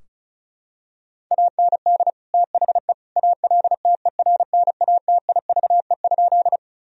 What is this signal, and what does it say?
Morse code, heard as a keyed tone, about 36 wpm, 700 Hz. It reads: AND THE ALTERNATIVE?
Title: Sentences from Top 1000 Words - Review Part 4 - 35wpm